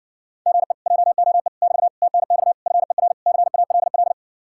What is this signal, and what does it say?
Morse code, heard as a keyed tone, about 60 wpm, 700 Hz. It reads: GE JOE <BT> TNX FER CALL